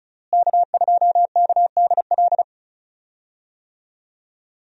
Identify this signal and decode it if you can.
Morse code, heard as a keyed tone, about 34 wpm, 700 Hz. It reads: K2KDL
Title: USA Call Signs - 5 Characters - Repeated 2x - 35wpm